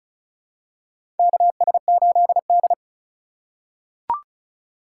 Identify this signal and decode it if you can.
Morse code, heard as a keyed tone, about 36 wpm, 700 Hz. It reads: KS8D
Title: USA Call Signs - 4x Character Spacing - 35wpm